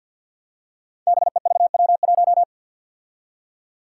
Morse code, heard as a keyed tone, about 47 wpm, 700 Hz. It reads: BEVW1
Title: Five Challenging Letter-Number Combinations 50wpm